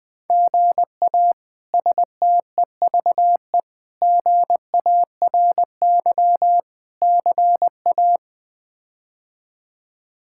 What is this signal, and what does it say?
Morse code, heard as a keyed tone, about 21 wpm, 700 Hz. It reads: GA STEVE GARY CA
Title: K1USN SST QSO Practice - 20wpm